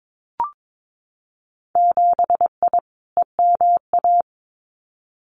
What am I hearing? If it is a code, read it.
Morse code, heard as a keyed tone, about 22 wpm, 700 Hz. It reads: 7I EMA
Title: Winter Field Day Exchanges Encore 22wpm